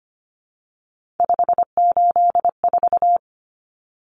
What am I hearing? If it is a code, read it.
Morse code, heard as a keyed tone, about 25 wpm, 700 Hz. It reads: 584
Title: Numbers - 3-Digit 25wpm